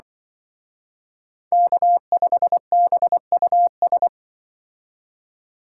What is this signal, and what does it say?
Morse code, heard as a keyed tone, about 24 wpm, 700 Hz. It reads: K5BUS
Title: Speed Racing - USA 1x3 Call Signs 20wpm